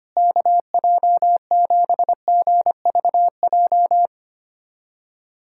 Morse code, heard as a keyed tone, about 25 wpm, 700 Hz. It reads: KJ7GVJ
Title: USA Callsign DE Callsign 25wpm